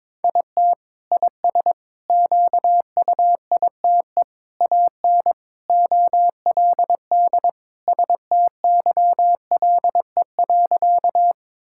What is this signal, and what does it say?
Morse code, heard as a keyed tone, about 22 wpm, 700 Hz. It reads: IT IS QUITE AN OLD STYLE.
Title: Sentences from Top 1000 Words - Review Part 5 - 22wpm